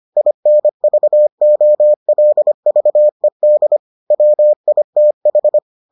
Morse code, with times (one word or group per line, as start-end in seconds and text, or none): 0.2-3.8 INVOLVED
4.1-5.6 WITH